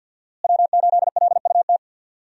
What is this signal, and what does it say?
Morse code, heard as a keyed tone, about 51 wpm, 700 Hz. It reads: W8LUT
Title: USA Call Signs - 5 Characters - Repeated 3x - 50wpm